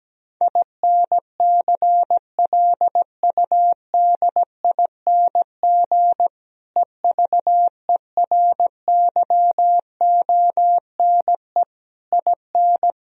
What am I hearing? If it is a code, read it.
Morse code, heard as a keyed tone, about 17 wpm, 700 Hz. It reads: INCLUDING EVERYONE IN
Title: Sentences from Top 1000 Words - Review Part 4 - 17wpm